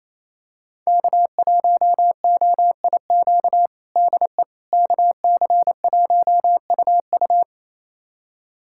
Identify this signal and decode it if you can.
Morse code, heard as a keyed tone, about 28 wpm, 700 Hz. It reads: K1OIQ DE KC1UU
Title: USA Callsign DE Callsign - 5 Characters - 28wpm